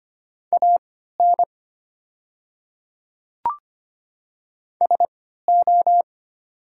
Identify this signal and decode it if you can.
Morse code, heard as a keyed tone, about 25 wpm, spaced as for 14 wpm, 700 Hz. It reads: AN SO
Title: 3x Character Spacing - Top 200 Words - 25wpm